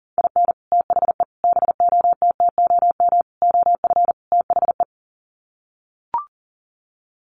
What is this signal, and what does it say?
Morse code, heard as a keyed tone, about 40 wpm, 700 Hz. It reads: IN THE BOTTOM OF THE
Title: Sets of 5 Words Encore 40wpm